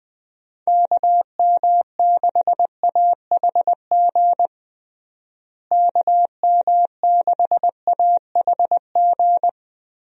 Morse code, heard as a keyed tone, about 21 wpm, 700 Hz. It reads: KM6AHG KM6AHG
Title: USA Call Signs - 6 Characters - Repeated 2x - 20wpm